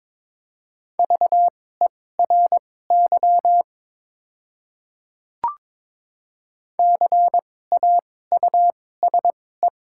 Morse code, heard as a keyed tone, about 22 wpm, spaced as for 16 wpm, 700 Hz. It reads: VERY CAUSE
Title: 2x Character Spacing - Top 300 Words - 22wpm